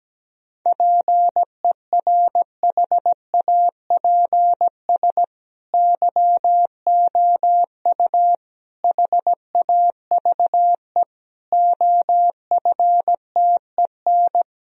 Morse code, heard as a keyed tone, about 17 wpm, 700 Hz. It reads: PERHAPS YOU HAVE OFTEN
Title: Sentences from Top 1000 Words - Review Part 1 - 17wpm